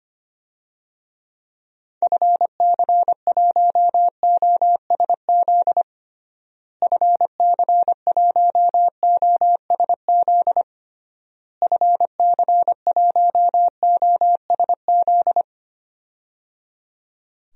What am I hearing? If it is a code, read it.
Morse code, heard as a keyed tone, about 25 wpm, 700 Hz. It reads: FC1OSZ FC1OSZ FC1OSZ
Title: DXCC-100 Call Signs - 6 Characters - Repeated 3x - 25wpm